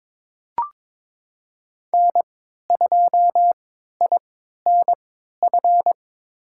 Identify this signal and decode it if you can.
Morse code, heard as a keyed tone, about 22 wpm, spaced as for 12 wpm, 700 Hz. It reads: N2INF
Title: USA Call Signs - 3x Character Spacing - 22wpm